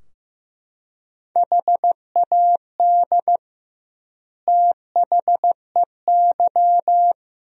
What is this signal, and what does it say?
Morse code, heard as a keyed tone, about 15 wpm, 700 Hz. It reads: HAD THEY
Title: 2x Word-Spacing - Sentences from Top 400 Words - 15wpm